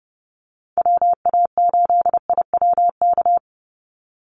Morse code, heard as a keyed tone, about 29 wpm, 700 Hz. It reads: WA8IWK
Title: Call Signs - 30wpm